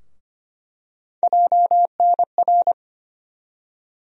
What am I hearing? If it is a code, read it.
Morse code, heard as a keyed tone, about 25 wpm, 700 Hz. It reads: JNR
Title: USA Call Sign Postfixes - 3 Letters 25wpm